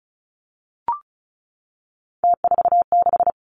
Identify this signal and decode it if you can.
Morse code, heard as a keyed tone, about 36 wpm, 700 Hz. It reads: T46
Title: Three Challenging Letter-Number Combinations 35wpm